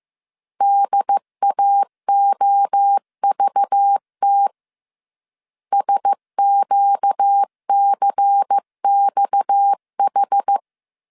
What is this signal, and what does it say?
Morse code, heard as a keyed tone, about 15 wpm, 800 Hz. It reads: DAOVT SQCXH
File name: CW / DET / 1 6 1500C S C 14wpm 2951